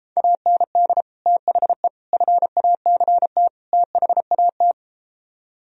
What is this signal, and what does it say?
Morse code, heard as a keyed tone, about 34 wpm, 700 Hz. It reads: AND THE FACT THAT